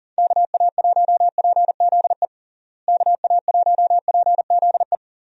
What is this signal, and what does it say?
Morse code, heard as a keyed tone, about 40 wpm, 700 Hz. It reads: KA1PZE KA1PZE
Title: USA Call Signs - 6 Characters - Repeated 3x - 40wpm